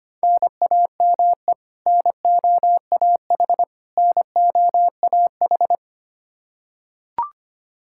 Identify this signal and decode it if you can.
Morse code, heard as a keyed tone, about 25 wpm, 700 Hz. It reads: NAME NOAH NOAH
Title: Practice QSO - Exchange 1 - Beginning - 25wpm